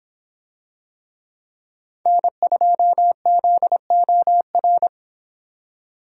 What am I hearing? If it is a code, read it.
Morse code, heard as a keyed tone, about 24 wpm, 700 Hz. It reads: N2ZOR